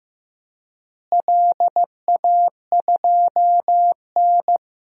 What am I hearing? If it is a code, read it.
Morse code, heard as a keyed tone, about 15 wpm, 700 Hz. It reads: LA2N
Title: DXCC-100 Call Signs - 4 Characters - Repeated 2x - 15wpm